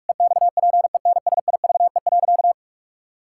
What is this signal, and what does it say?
Morse code, heard as a keyed tone, about 46 wpm, 700 Hz. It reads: EXPENSIVE.
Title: Pangram Challenge v2 - 45wpm